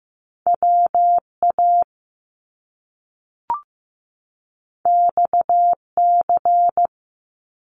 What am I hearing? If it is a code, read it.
Morse code, heard as a keyed tone, about 15 wpm, 700 Hz. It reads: WA XC
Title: Top Two Letter Combinations in Words 15wpm